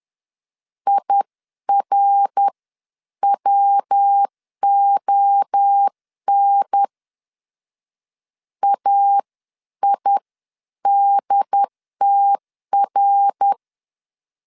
Morse code, heard as a keyed tone, about 10 wpm, spaced as for 9 wpm, 800 Hz. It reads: IRWON AIDTR
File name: 10wpm